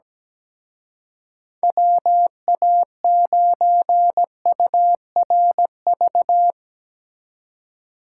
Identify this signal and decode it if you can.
Morse code, heard as a keyed tone, about 19 wpm, 700 Hz. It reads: WA9URV
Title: Speed Racing - USA 2x3 Call Signs 17wpm